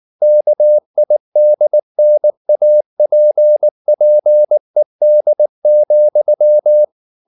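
Morse code, heard as a keyed tone, about 19 wpm, 600 Hz. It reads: KIDNAPPED,
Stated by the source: Morse code at 19 wpm from CW med Gnister, reading Ken Follett's novel Never in English.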